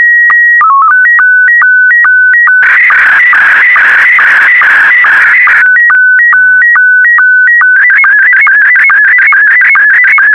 SSTV (slow-scan television), Scottie 1.